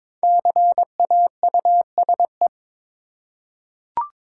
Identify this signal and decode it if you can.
Morse code, heard as a keyed tone, about 22 wpm, 690 Hz. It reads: CAUSE
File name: 5 Letter Words - First 3 Letters Given - 22wpm